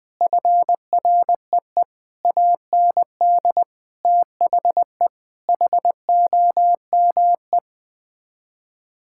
Morse code, heard as a keyed tone, about 20 wpm, 700 Hz. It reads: FREE AND THE HOME
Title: Sets of 8 Words - 20wpm